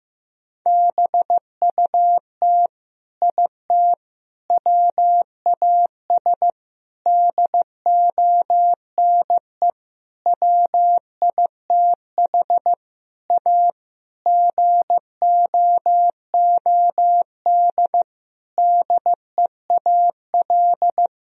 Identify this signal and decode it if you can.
Morse code, heard as a keyed tone, about 16 wpm, 700 Hz. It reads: BUT IT WAS DONE WITH A GOOD DEAL
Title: Sentences from Top 1000 Words - Review Part 4 - 15wpm